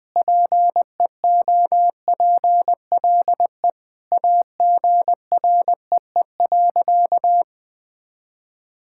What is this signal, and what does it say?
Morse code, heard as a keyed tone, about 19 wpm, 700 Hz. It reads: PEOPLE AGREE.
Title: Sentences with 5-Word Phrases - 20wpm